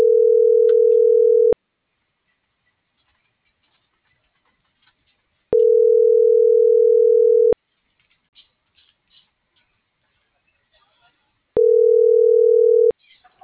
Ringback tone (North America), 440 + 480 Hz.